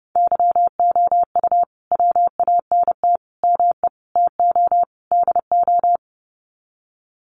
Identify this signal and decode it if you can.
Morse code, heard as a keyed tone, about 30 wpm, 700 Hz. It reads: YOU WANT ME TO DO
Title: Sets of 5 Words Encore2 30wpm